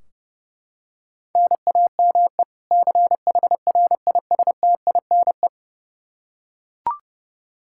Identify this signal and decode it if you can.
Morse code, heard as a keyed tone, about 32 wpm, 700 Hz. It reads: NAME CHRISTINE
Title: Practice QSO - Exchange 1 - Beginning - 30wpm